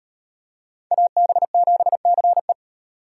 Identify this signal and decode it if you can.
Morse code, heard as a keyed tone, about 37 wpm, 700 Hz. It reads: AB7CE